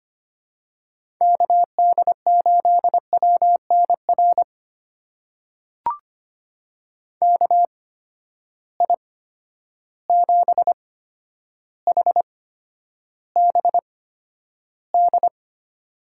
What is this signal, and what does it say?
Morse code, heard as a keyed tone, about 25 wpm, spaced as for 11 wpm, 700 Hz. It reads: KD8WNR KI7HBD
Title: USA Call Signs - 6 Characters - 8x Character Spacing - 25wpm